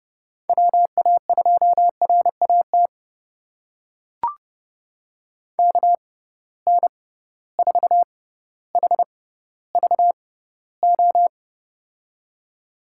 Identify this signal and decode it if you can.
Morse code, heard as a keyed tone, about 31 wpm, spaced as for 16 wpm, 700 Hz. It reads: WA2RAT KN4HVO
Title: USA Call Signs - 6 Characters - 6x Character Spacing - 30wpm